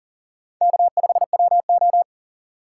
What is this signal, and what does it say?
Morse code, heard as a keyed tone, about 40 wpm, 700 Hz. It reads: K5WO
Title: USA Call Signs - 4 Characters - 6x Character Spacing - 40wpm